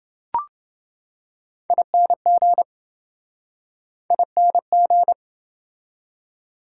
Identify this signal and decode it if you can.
Morse code, heard as a keyed tone, about 28 wpm, 700 Hz. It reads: ING ING